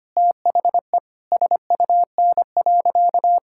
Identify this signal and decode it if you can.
Morse code, heard as a keyed tone, about 25 wpm, 700 Hz. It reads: THE SUN.
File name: practice / 25wpm / Difficult Sentences with Hints - 25wpm